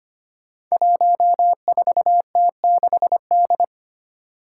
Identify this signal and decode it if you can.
Morse code, heard as a keyed tone, about 25 wpm, 700 Hz. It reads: 14T6D
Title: Five Challenging Letter-Number Combinations 25wpm